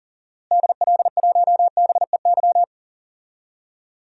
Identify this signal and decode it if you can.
Morse code, heard as a keyed tone, about 40 wpm, 700 Hz. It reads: DL1BEY
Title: DXCC-100 Call Signs - 6 Characters - Repeated 2x - 40wpm